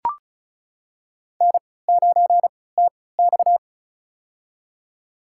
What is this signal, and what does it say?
Morse code, heard as a keyed tone, about 34 wpm, spaced as for 20 wpm, 700 Hz. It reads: N9TX
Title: USA Call Signs - 4 Characters - 3x Character Spacing - 35wpm